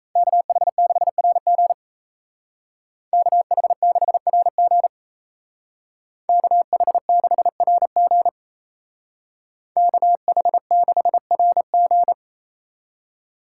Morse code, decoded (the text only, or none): KH6RG KH6RG KH6RG KH6RG